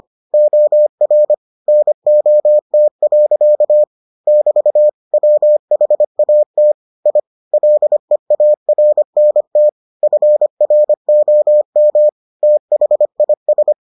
Morse code, heard as a keyed tone, about 25 wpm, 600 Hz. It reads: OR NOT. <BT> WHAT I LEARNT FROM THIS